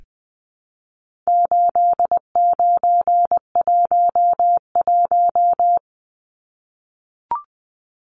Morse code, heard as a keyed tone, about 20 wpm, 700 Hz. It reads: 8911
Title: Numbers - 4-Digit 20wpm